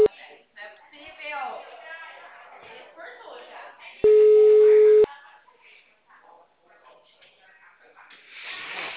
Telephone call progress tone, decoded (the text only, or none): ringback tone